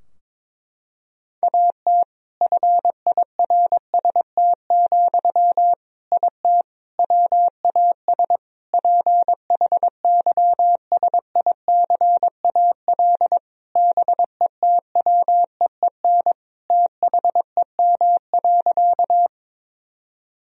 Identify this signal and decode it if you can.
Morse code, heard as a keyed tone, about 22 wpm, 700 Hz. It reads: AT FIRST, IT WAS PHYSICAL BETWEEN THEM.
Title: Sentences from Top 1000 Words - Review Part 4 - 22wpm